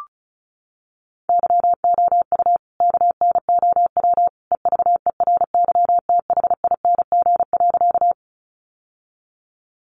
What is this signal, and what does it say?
Morse code, heard as a keyed tone, about 34 wpm, 700 Hz. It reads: YOU KNOW EVERYTHING.
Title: Sentences from Top 700 Words 35wpm